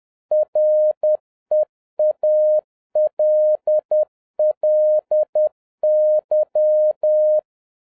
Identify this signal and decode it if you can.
Morse code, heard as a keyed tone, about 10 wpm, 610 Hz. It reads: REALLY